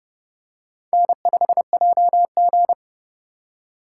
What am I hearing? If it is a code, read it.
Morse code, heard as a keyed tone, about 31 wpm, 700 Hz. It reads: N5JG